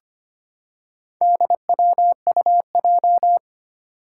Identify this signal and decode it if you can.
Morse code, heard as a keyed tone, about 25 wpm, 700 Hz. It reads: DWUJ
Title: Four Challenging Letter-Number Combinations 25wpm